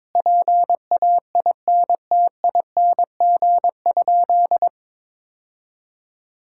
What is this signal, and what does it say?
Morse code, heard as a keyed tone, about 21 wpm, 700 Hz. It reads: PAINTING?
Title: Sentences from Top 1000 Words - Review Part 6 - 22wpm